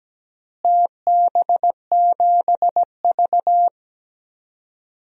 Morse code, heard as a keyed tone, about 17 wpm, 700 Hz. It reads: TB7V
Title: DXCC-100 Call Signs - 4 Characters - Repeated 3x - 17wpm